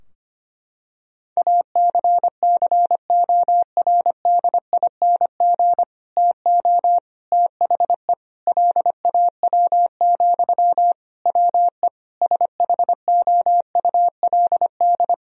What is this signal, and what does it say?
Morse code, heard as a keyed tone, about 25 wpm, 700 Hz. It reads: ACCORDING TO THE LAW, WE SHOULD
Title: Sentences from Top 1000 Words - Review Part 4 - 25wpm